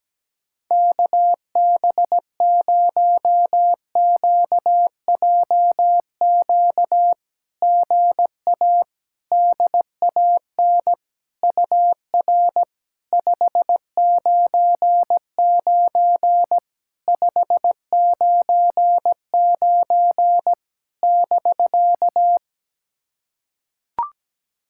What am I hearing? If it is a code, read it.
Morse code, heard as a keyed tone, about 17 wpm, 700 Hz. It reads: KB0QJQ GA DAN UR 599 599 <BK>
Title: SOTA QSO Practice 17wpm